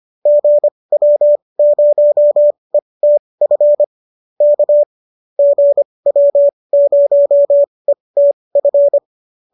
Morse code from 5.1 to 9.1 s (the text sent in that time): GW0ETF